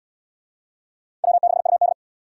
Morse code, heard as a keyed tone, about 67 wpm, 700 Hz. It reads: W6UD